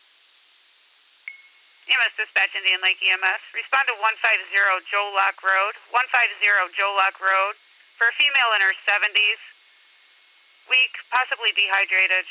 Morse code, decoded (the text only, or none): none